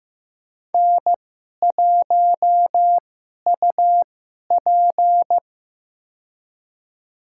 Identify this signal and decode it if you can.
Morse code, heard as a keyed tone, about 15 wpm, spaced as for 11 wpm, 700 Hz. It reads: N1UP
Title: USA Call Signs - 4 Characters - 2x Character Spacing - 15wpm